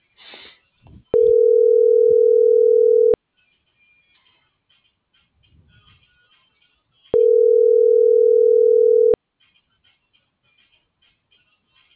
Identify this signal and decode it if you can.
Ringback tone (North America), 440 + 480 Hz.